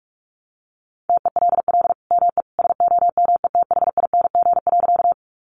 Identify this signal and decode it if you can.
Morse code, heard as a keyed tone, about 44 wpm, 700 Hz. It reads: TELL ME SOMETHING.
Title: Sentences from Top 600 Words 45wpm